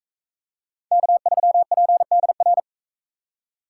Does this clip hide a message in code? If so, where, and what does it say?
Yes: Morse code from 0.9 to 2.6 s, heard as a keyed tone, about 44 wpm, 700 Hz. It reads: K3PDR